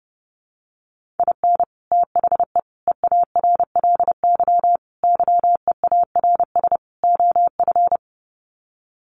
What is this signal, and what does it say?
Morse code, heard as a keyed tone, about 28 wpm, 700 Hz. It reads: IN THE EARLY YEARS OF
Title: Sets of 5 Words Encore 30wpm